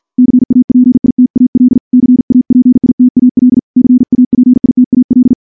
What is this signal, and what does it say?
Morse code, heard as a keyed tone, about 47 wpm, 250 Hz. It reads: CAPITAL CAPITAL CAPITAL
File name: morse / morse-training-50WPM